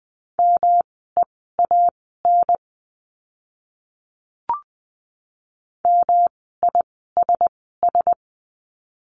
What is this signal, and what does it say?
Morse code, heard as a keyed tone, about 20 wpm, spaced as for 14 wpm, 700 Hz. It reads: MEAN MISS
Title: 108 Lesson 18f - Letter m - Words 20wpm